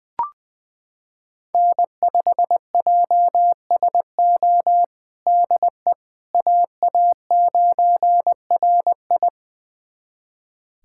Morse code, heard as a keyed tone, about 20 wpm, 700 Hz. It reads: N5JSO DE AA9RI
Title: USA Callsign DE Callsign - 5 Characters - 20wpm